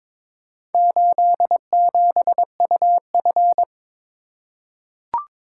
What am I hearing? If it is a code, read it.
Morse code, heard as a keyed tone, about 21 wpm, 700 Hz. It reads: 87UF
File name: practice / 22wpm / Four Letter-Number Combinations - 22wpm